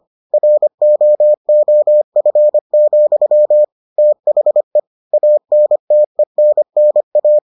Morse code, heard as a keyed tone, about 25 wpm, 600 Hz. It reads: ROOF, THE ANTENNA